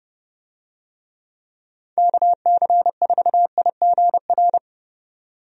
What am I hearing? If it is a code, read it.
Morse code, heard as a keyed tone, about 30 wpm, 700 Hz. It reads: KC4IGR
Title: USA Call Signs - 6 Characters - 30wpm